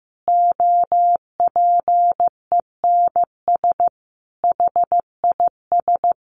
Morse code, heard as a keyed tone, about 15 wpm, 700 Hz. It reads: OPENS HIS
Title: Easy Phrases 15wpm